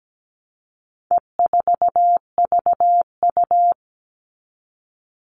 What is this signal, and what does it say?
Morse code, heard as a keyed tone, about 17 wpm, 700 Hz. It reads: E4VU
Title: All Cut Numbers - 4-Digit 17wpm